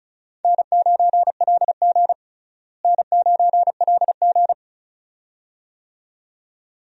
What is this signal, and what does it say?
Morse code, heard as a keyed tone, about 36 wpm, 700 Hz. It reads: N9LG N9LG
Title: USA Call Signs - 4 Characters - Repeated 2x - 35wpm